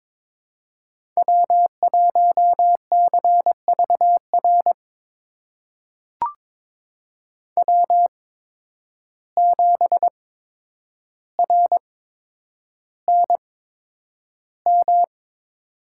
Morse code, heard as a keyed tone, about 22 wpm, spaced as for 9 wpm, 700 Hz. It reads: W1CVR W7RNM
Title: USA Call Signs - 5 Characters - 8x Character Spacing - 22wpm